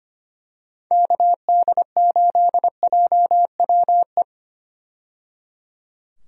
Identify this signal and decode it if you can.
Morse code, heard as a keyed tone, about 25 wpm, 700 Hz. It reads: KD8JWE
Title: USA Call Signs - 6 Characters - 25wpm